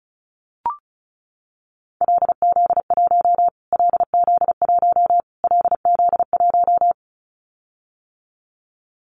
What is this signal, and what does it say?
Morse code, heard as a keyed tone, about 34 wpm, 700 Hz. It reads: LZ1 LZ1 LZ1
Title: Callsign Prefixes for Top 25 DXCC - Repeated 3x - 35wpm